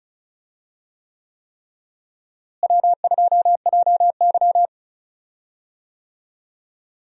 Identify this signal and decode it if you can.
Morse code, heard as a keyed tone, about 36 wpm, 700 Hz. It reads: W2JY
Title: USA Call Signs - 4 Characters - Repeated 3x - 35wpm